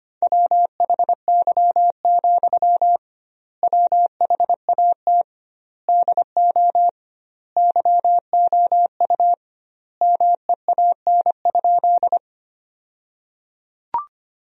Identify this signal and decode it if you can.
Morse code, heard as a keyed tone, about 25 wpm, 700 Hz. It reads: WHY, WHAT DO YOU MEAN?
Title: 2x Word-Spacing - Sentences from Top 200 Words - 25wpm